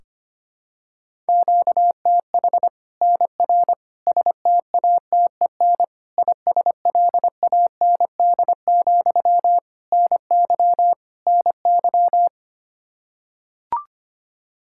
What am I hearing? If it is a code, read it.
Morse code, heard as a keyed tone, about 25 wpm, 700 Hz. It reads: QTH NR STATEN ISLAND, NY NY
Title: Practice QSO - Exchange 1 - Beginning - 25wpm